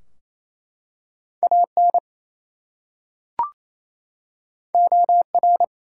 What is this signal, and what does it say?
Morse code, heard as a keyed tone, about 27 wpm, 700 Hz. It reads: AN OR
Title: Two Letter Words - 28wpm